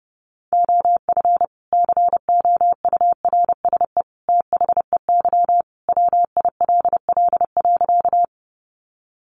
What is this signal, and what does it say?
Morse code, heard as a keyed tone, about 28 wpm, 700 Hz. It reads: OF COURSE THEY WILL.
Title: Sentences from Top 500 Words 30wpm